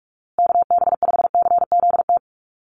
Morse code, heard as a keyed tone, about 44 wpm, 700 Hz. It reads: KB5CZT
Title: USA Call Signs Repeated 2x 45wpm